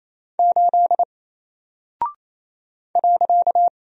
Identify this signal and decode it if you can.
Morse code, heard as a keyed tone, about 27 wpm, 700 Hz. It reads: 8 .
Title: Single Character (Letter:Number:Punctuation:Prosign) 28wpm